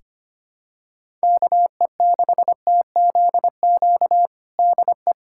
Morse code, heard as a keyed tone, about 25 wpm, 700 Hz. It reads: KE6TZQ DE
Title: USA Callsign DE Callsign 25wpm